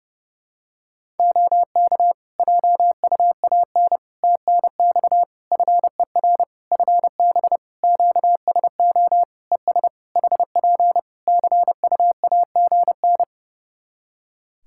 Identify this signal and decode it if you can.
Morse code, heard as a keyed tone, about 30 wpm, 700 Hz. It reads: OK JUAN TNX FER FB QSO ES HP CUAGN
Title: K1USN SST QSO Practice 30wpm